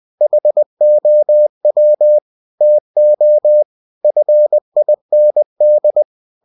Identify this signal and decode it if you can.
Morse code, heard as a keyed tone, about 20 wpm, 600 Hz. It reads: HOW TO FIND
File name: cw-books / winnie-the-pooh / winnie-20wpm-page-0029